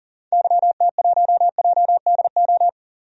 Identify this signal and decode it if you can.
Morse code, heard as a keyed tone, about 40 wpm, 700 Hz. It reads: YT1JDO
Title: DXCC-100 Call Signs - 6 Characters - 40wpm